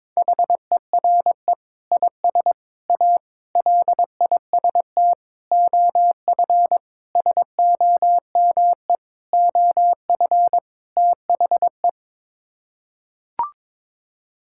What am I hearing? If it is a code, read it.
Morse code, heard as a keyed tone, about 22 wpm, 700 Hz. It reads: HERE IS A LIST OF SOME OF THE
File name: practice / 22wpm / Sets of 8 Words - 22wpm